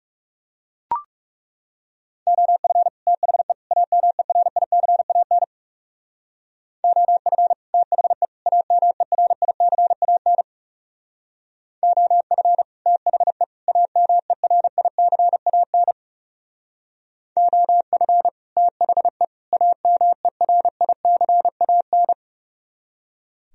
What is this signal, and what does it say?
Morse code, heard as a keyed tone, about 38 wpm, 700 Hz. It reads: OF THE AMERICAN OF THE AMERICAN OF THE AMERICAN OF THE AMERICAN